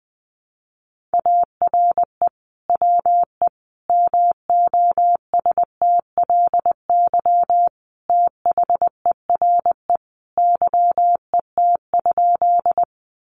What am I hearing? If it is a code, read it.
Morse code, heard as a keyed tone, about 20 wpm, 700 Hz. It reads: ARE WE MOSTLY THERE YET?
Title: Sentences from Top 1300 Words 20wpm